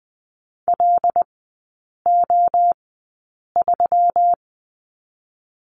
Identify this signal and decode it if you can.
Morse code, heard as a keyed tone, about 20 wpm, spaced as for 8 wpm, 700 Hz. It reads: LO3
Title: 127 Lesson 21e - Number 6 - Three Characters 20wpm